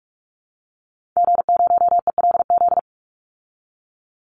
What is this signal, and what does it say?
Morse code, heard as a keyed tone, about 44 wpm, 700 Hz. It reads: G0ELZ
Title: Call Signs - 45wpm